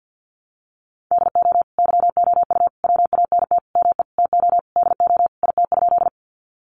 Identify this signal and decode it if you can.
Morse code, heard as a keyed tone, about 50 wpm, 700 Hz. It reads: DO YOU WANT ME TO DO IT?
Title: Sentences from Top 200 Words 50wpm